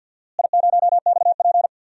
Morse code, heard as a keyed tone, about 48 wpm, 700 Hz. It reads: I0XP
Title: DXCC-100 Call Signs - 4 Characters - Repeated 3x - 50wpm